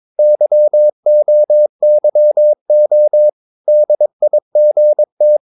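Morse code, heard as a keyed tone, about 22 wpm, 600 Hz. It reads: YOYO DIGT
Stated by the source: Morse practice of short Danish words at 22 wpm, from CW med Gnister.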